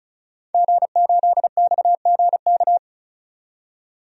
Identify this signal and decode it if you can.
Morse code, heard as a keyed tone, about 35 wpm, 700 Hz. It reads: G8XGK
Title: DXCC-100 Call Signs - 5 Characters - 35wpm